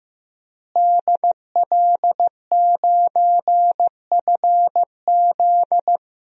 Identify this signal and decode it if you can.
Morse code, heard as a keyed tone, about 15 wpm, 700 Hz. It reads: DL9FZ
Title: DXCC-100 Call Signs - 5 Characters - Repeated 3x - 15wpm